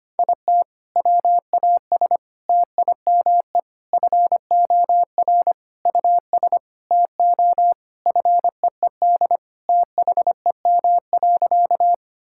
Morse code, heard as a keyed tone, about 25 wpm, 700 Hz. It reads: IT WAS TIME FOR US TO FEED THEM.
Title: Sentences from Top 1600 Words 25wpm